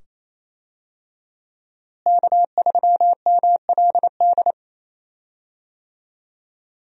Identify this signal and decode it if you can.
Morse code, heard as a keyed tone, about 28 wpm, 700 Hz. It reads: K3MLD
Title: USA Call Signs - 5 Characters - Repeated 2x - 28wpm